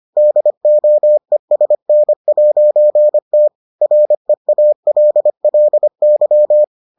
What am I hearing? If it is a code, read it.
Morse code, heard as a keyed tone, about 25 wpm, 600 Hz. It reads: DOESN'T REALLY